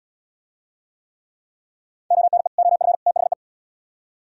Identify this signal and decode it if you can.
Morse code, heard as a keyed tone, about 77 wpm, 700 Hz. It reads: ONE OF THE